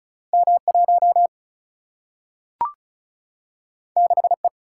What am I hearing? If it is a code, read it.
Morse code, heard as a keyed tone, about 36 wpm, 700 Hz. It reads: M 1 6 E
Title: Single-Letter Single-Number 35wpm